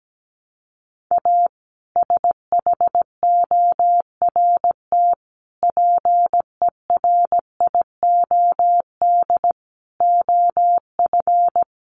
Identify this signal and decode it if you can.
Morse code, heard as a keyed tone, about 17 wpm, 700 Hz. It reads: A SHORT PERIOD OF
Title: Sets of 5 Words Encore 17wpm